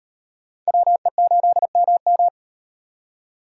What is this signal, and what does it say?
Morse code, heard as a keyed tone, about 38 wpm, 700 Hz. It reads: WE8MM